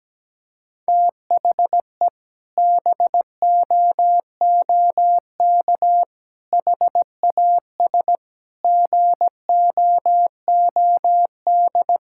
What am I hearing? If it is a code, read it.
Morse code, heard as a keyed tone, about 17 wpm, 700 Hz. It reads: THE BOOK HAS GOOD
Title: Sentences from Top 1000 Words - Review Part 6 - 17wpm